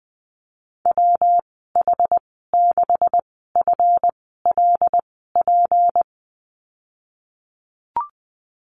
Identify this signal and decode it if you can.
Morse code, heard as a keyed tone, about 20 wpm, spaced as for 15 wpm, 700 Hz. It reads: WH6FLP
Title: 231 Lesson 37g - Letter x - Call Signs 20wpm